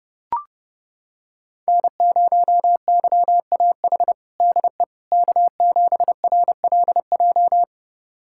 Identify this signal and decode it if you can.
Morse code, heard as a keyed tone, about 30 wpm, 700 Hz. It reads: N0YAH DE K7RLJ
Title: USA Callsign DE Callsign - 5 Characters - 30wpm